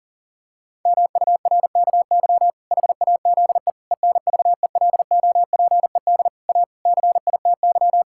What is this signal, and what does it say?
Morse code, heard as a keyed tone, about 40 wpm, 700 Hz. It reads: MURKY HAZE ENVELOPED A CITY
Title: Pangram Challenge v2 - 40wpm